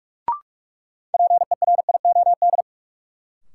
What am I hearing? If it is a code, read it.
Morse code, heard as a keyed tone, about 46 wpm, 700 Hz. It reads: PERIOD
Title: Sets of 5 Words - New Word Drill - 45wpm